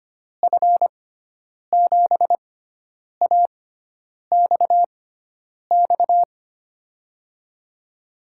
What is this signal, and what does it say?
Morse code, heard as a keyed tone, about 25 wpm, spaced as for 9 wpm, 700 Hz. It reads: F7AXX
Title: USA Call Signs - 6 Characters - 6x Character Spacing - 25wpm